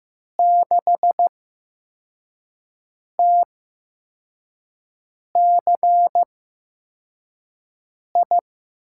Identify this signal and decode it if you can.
Morse code, heard as a keyed tone, about 15 wpm, spaced as for 4 wpm, 700 Hz. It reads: 6TCI
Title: USA Call Signs - 8x Character Spacing - 15wpm